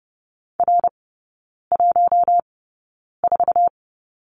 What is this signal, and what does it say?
Morse code, heard as a keyed tone, about 30 wpm, spaced as for 9 wpm, 690 Hz. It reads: R14
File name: Lesson 16e - Letter c - Three Characters 30wpm